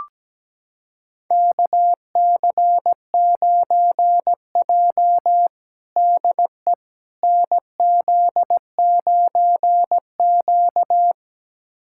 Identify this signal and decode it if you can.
Morse code, heard as a keyed tone, about 17 wpm, 700 Hz. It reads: KC9J DE NZ9Q
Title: USA Callsign DE Callsign - 4 Characters - 17wpm